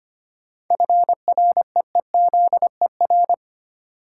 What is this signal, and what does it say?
Morse code, heard as a keyed tone, about 25 wpm, 700 Hz. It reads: FREEZER
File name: practice / 25wpm / Easy Words with Hints - 25wpm